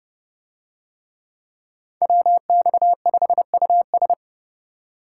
Morse code, heard as a keyed tone, about 29 wpm, 700 Hz. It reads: WX5US